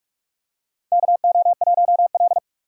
Morse code, heard as a keyed tone, about 46 wpm, 700 Hz. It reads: KO1L
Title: USA Call Signs - 6x Character Spacing - 45wpm